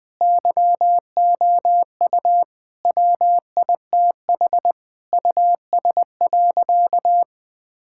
Morse code, heard as a keyed tone, about 20 wpm, 700 Hz. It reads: YOU WITH US.
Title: Sentences with 5-Word Phrases - 20wpm